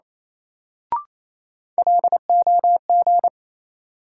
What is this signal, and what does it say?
Morse code, heard as a keyed tone, about 28 wpm, 700 Hz. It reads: LOG
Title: Sentences from Top 1400 Words - New Word Drill Alt - 28wpm